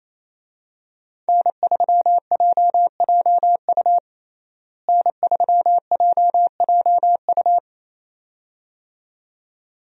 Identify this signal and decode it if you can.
Morse code, heard as a keyed tone, about 28 wpm, 700 Hz. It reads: N3JJU N3JJU
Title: USA Call Signs - 5 Characters - Repeated 2x - 28wpm